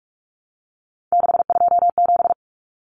Morse code, heard as a keyed tone, about 44 wpm, 700 Hz. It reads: 627
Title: Numbers - 3-Digit 45wpm